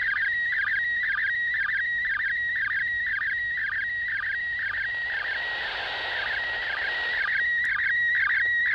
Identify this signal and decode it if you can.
SSTV (slow-scan television), PD 120.
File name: spazio / ISS 145 800MHz SSTV 5